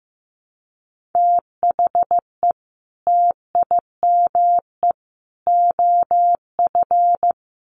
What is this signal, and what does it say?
Morse code, heard as a keyed tone, about 15 wpm, 700 Hz. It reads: THE TIME OF
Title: Sets of 3 Words 15wpm